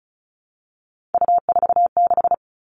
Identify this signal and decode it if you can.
Morse code, heard as a keyed tone, about 36 wpm, 700 Hz. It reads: U46
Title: Three Challenging Letter-Number Combinations 35wpm